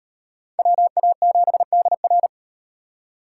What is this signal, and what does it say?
Morse code, heard as a keyed tone, about 38 wpm, 700 Hz. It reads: WA7DR